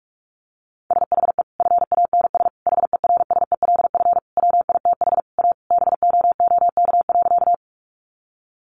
Morse code, heard as a keyed tone, about 44 wpm, 700 Hz. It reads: SHE FANS HERSELF WITH A BOOK.
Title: Additional Easy Phrases 45wpm